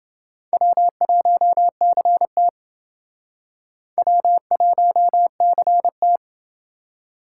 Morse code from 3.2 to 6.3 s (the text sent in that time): W1CT